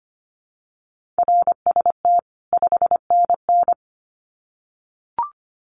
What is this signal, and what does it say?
Morse code, heard as a keyed tone, about 25 wpm, 700 Hz. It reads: RST 5NN
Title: Common QSO Phrases 25wpm